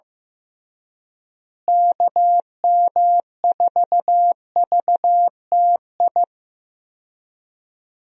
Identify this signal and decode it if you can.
Morse code, heard as a keyed tone, about 16 wpm, 700 Hz. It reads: KM4VTI
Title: USA Call Signs - 6 Characters - Repeated 2x - 15wpm